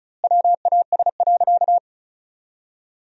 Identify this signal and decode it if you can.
Morse code, heard as a keyed tone, about 36 wpm, 700 Hz. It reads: WAS.
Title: 2x Word-Spacing - Sentences from Top 300 Words - 35wpm